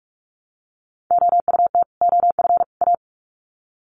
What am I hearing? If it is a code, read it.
Morse code, heard as a keyed tone, about 44 wpm, 700 Hz. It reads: OUT OF A
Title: Sets of 3 Words 45wpm